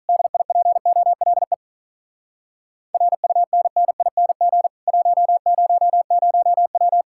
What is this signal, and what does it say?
Morse code, heard as a keyed tone, about 48 wpm, 700 Hz. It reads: DIPOLE RUNNING 100W